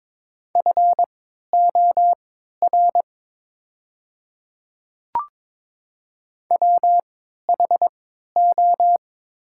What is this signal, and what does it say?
Morse code, heard as a keyed tone, about 22 wpm, spaced as for 12 wpm, 700 Hz. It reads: FOR WHO
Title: 3x Character Spacing - Top 300 Words - 22wpm